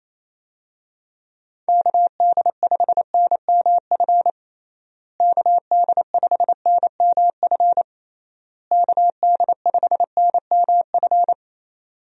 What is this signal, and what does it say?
Morse code, heard as a keyed tone, about 28 wpm, 700 Hz. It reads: KD5NMF KD5NMF KD5NMF
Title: USA Call Signs - 6 Characters - Repeated 3x - 28wpm